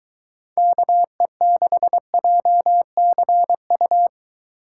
Morse code, heard as a keyed tone, about 23 wpm, 700 Hz. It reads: KE6JCU